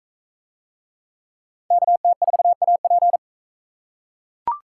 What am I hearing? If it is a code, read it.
Morse code, heard as a keyed tone, about 39 wpm, 700 Hz. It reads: KT4AP